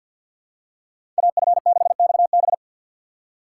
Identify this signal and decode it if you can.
Morse code, heard as a keyed tone, about 52 wpm, 700 Hz. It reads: AF6XB
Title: USA Call Signs - 5 Characters - 4x Character Spacing - 50wpm